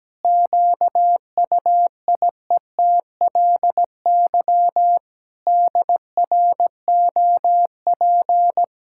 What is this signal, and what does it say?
Morse code, heard as a keyed tone, about 17 wpm, 700 Hz. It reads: QUIETLY DROP
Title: Pangram Challenge v2 - 17wpm